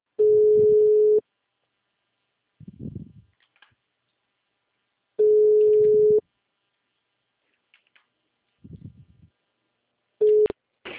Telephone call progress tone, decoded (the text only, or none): ringback tone